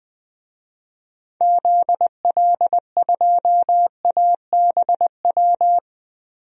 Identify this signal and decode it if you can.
Morse code, heard as a keyed tone, about 20 wpm, 700 Hz. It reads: ZL2ABW
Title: DXCC-100 Call Signs - 6 Characters - 20wpm